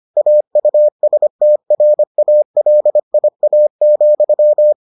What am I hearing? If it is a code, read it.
Morse code, heard as a keyed tone, about 25 wpm, 600 Hz. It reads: AUSTRALIA,